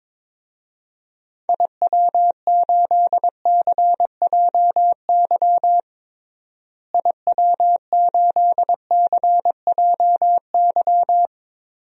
Morse code, heard as a keyed tone, about 22 wpm, 700 Hz. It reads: IW8CJY IW8CJY
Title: DXCC-100 Call Signs - 6 Characters - Repeated 2x - 22wpm